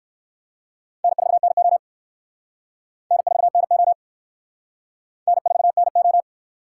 Morse code, heard as a keyed tone, about 59 wpm, 700 Hz. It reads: N4NO N4NO N4NO